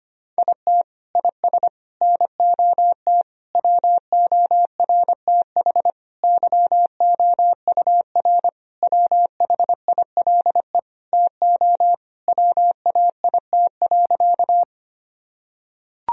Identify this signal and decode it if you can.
Morse code, heard as a keyed tone, about 25 wpm, 700 Hz. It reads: IT IS NOT WORTH YOUR WHILE TO WAIT.
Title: Sentences from Top 1000 Words - Review Part 1 - 25wpm